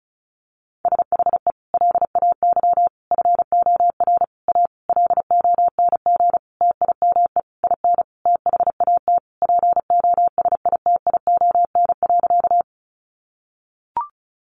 Morse code, heard as a keyed tone, about 34 wpm, 700 Hz. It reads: SHE LAY FOR A LONG TIME IN THAT POSITION.
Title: Sentences from Top 900 Words 35wpm